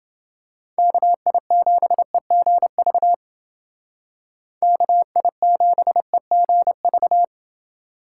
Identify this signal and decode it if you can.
Morse code, heard as a keyed tone, about 29 wpm, 700 Hz. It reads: KI7EGV KI7EGV